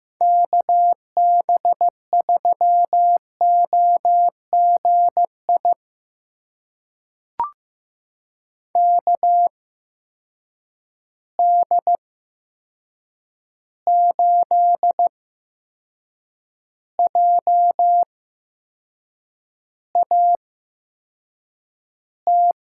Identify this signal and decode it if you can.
Morse code, heard as a keyed tone, about 15 wpm, spaced as for 6 wpm, 700 Hz. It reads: KB3OGI KD8JAT
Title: USA Call Signs - 6 Characters - 8x Character Spacing - 15wpm